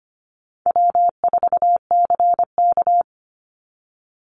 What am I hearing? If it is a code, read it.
Morse code, heard as a keyed tone, about 25 wpm, 700 Hz. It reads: W4CK
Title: Easy Call Signs Repeated 2x - 25wpm